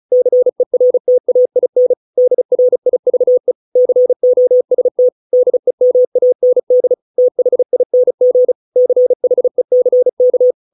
Morse code, heard as a keyed tone, about 35 wpm, 500 Hz. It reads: CERTAIN DRIVE COST DEMAND THING CHECK